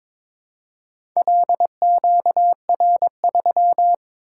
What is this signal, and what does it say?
Morse code, heard as a keyed tone, about 22 wpm, 700 Hz. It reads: LQR3
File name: practice / 22wpm / Four Letter-Number Combinations - 22wpm